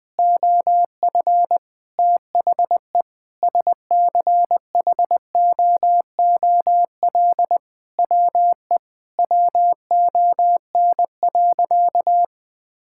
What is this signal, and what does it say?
Morse code, heard as a keyed tone, about 20 wpm, 700 Hz. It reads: OF THE SCHOOL WE WON.
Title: Sentences with 9-Word Phrases - 20wpm